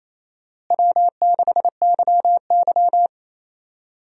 Morse code, heard as a keyed tone, about 28 wpm, 700 Hz. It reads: W6YY
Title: USA Call Signs - 4 Characters - Repeated 2x - 28wpm